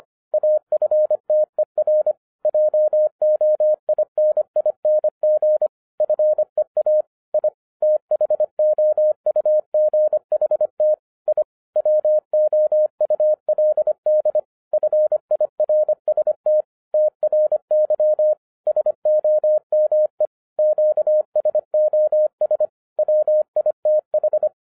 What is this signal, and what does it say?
Morse code, heard as a keyed tone, about 25 wpm, 610 Hz. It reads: AFTER JOINING FEA I THOUGHT I WOULD FIRST TRY SOME QSOS WITH
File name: 2012-01-mp3 / n06-FIRST FISTS CONTACT-25wpm